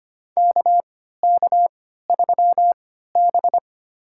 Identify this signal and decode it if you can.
Morse code, heard as a keyed tone, about 25 wpm, spaced as for 14 wpm, 700 Hz. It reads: KK3B
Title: USA Call Signs - 4 Characters - 3x Character Spacing - 25wpm